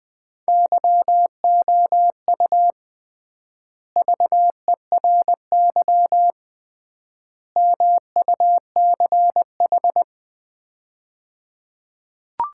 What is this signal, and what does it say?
Morse code, heard as a keyed tone, about 20 wpm, 700 Hz. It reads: YOU VERY MUCH
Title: 3x Word-Spacing - Sets of 3 Words - 20wpm